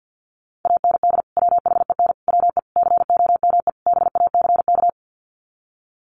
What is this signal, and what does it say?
Morse code, heard as a keyed tone, about 50 wpm, 700 Hz. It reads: AND WHEN WE COME BACK
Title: Sets of 5 Words 50wpm